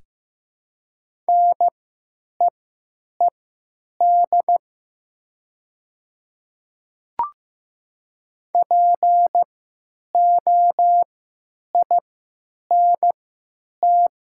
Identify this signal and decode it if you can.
Morse code, heard as a keyed tone, about 16 wpm, spaced as for 9 wpm, 700 Hz. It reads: NEED POINT
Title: 3x Character Spacing - Top 200 Words - 15wpm